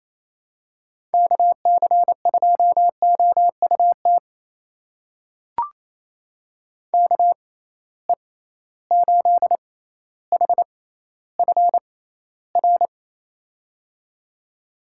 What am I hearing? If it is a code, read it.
Morse code, heard as a keyed tone, about 28 wpm, spaced as for 14 wpm, 700 Hz. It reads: KC2OUT KE8HFR
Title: USA Call Signs - 6 Characters - 6x Character Spacing - 28wpm